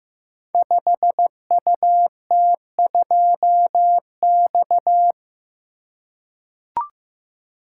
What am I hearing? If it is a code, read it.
Morse code, heard as a keyed tone, about 15 wpm, 700 Hz. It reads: 5UT2X
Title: Code Groups - 5 Letter:Numbers - 15wpm